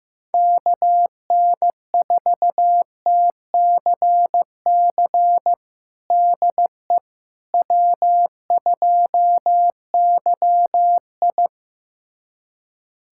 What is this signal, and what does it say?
Morse code, heard as a keyed tone, about 15 wpm, 700 Hz. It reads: KN4TCC DE W2YI
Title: USA Callsign DE Callsign 15wpm